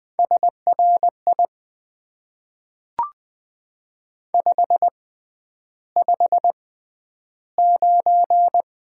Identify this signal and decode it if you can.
Morse code, heard as a keyed tone, about 20 wpm, spaced as for 10 wpm, 700 Hz. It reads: SRI 559
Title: Top 90 QSO Elements - 6x Character Spacing - 20wpm